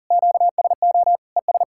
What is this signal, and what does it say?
Morse code, heard as a keyed tone, about 40 wpm, 700 Hz. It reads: QSO ES
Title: K1USN SST QSO Practice 40wpm